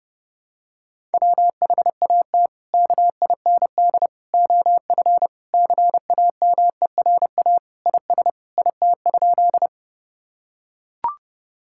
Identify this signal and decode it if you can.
Morse code, heard as a keyed tone, about 30 wpm, 700 Hz. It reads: WHAT KIND OF CAMERA IS IT?
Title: Sentences from Top 1000 Words - Review Part 5 - 30wpm